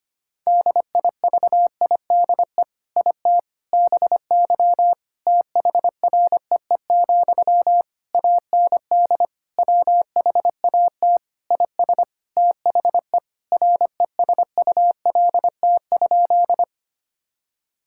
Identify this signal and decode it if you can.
Morse code, heard as a keyed tone, about 25 wpm, 700 Hz. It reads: DIVIDE IT BY THREE, AND WHAT IS THE RESULT?
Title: Sentences from Top 1600 Words 25wpm